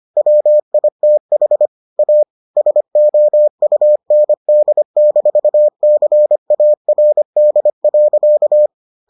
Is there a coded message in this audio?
Yes: Morse code from 0.2 to 8.7 s, heard as a keyed tone, about 25 wpm, 600 Hz. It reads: WITH A SOUND-CARD.